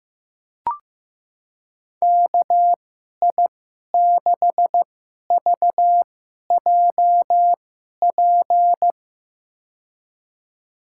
Morse code, heard as a keyed tone, about 16 wpm, spaced as for 11 wpm, 700 Hz. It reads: KI6VJP
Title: USA Call Signs - 6 Characters - 2x Character Spacing - 15wpm